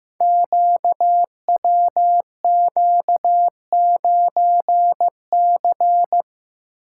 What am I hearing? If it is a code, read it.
Morse code, heard as a keyed tone, about 15 wpm, 700 Hz. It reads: QWQ9C
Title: Code Groups - 5 Letter:Numbers - 15wpm